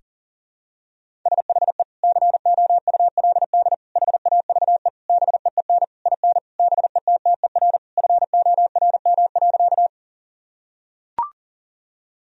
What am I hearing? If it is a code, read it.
Morse code, heard as a keyed tone, about 40 wpm, 700 Hz. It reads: SHE COULD HAVE BEEN IN BETTER FORM.
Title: Sentences from Top 1000 Words - Review Part 1 - 40wpm